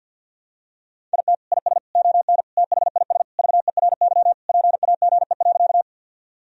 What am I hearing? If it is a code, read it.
Morse code, heard as a keyed tone, about 53 wpm, 700 Hz. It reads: IT IS ON THIS VERY PAGE.
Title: Sentences from Top 1000 Words - Review Part 1 - 50wpm